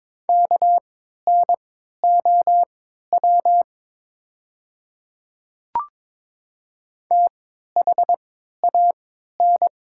Morse code, heard as a keyed tone, about 22 wpm, spaced as for 12 wpm, 700 Hz. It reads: KNOW THAN
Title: 3x Character Spacing - Top 100 Words - 22wpm